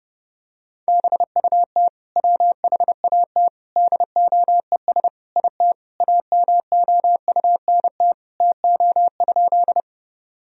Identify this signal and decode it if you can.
Morse code, heard as a keyed tone, about 30 wpm, 700 Hz. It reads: BUT WHAT DOES IT AMOUNT TO?
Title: Sentences from Top 1000 Words - Review Part 4 - 30wpm